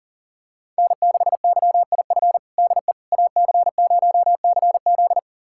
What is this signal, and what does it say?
Morse code, heard as a keyed tone, about 40 wpm, 700 Hz. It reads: N6YIF DE AC0CZ
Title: USA Callsign DE Callsign - 5 Characters - 40wpm